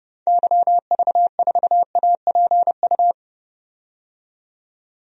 Morse code, heard as a keyed tone, about 32 wpm, 700 Hz. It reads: YV4APU
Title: DXCC-100 Call Signs - 6 Characters - 30wpm